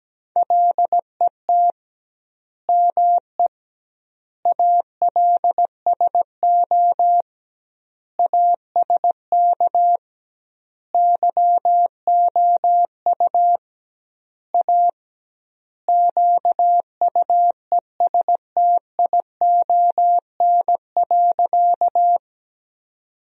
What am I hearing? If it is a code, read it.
Morse code, heard as a keyed tone, about 17 wpm, 700 Hz. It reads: LET ME ALSO ASK YOU A QUESTION.
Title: 2x Word-Spacing - Sentences from Top 400 Words - 17wpm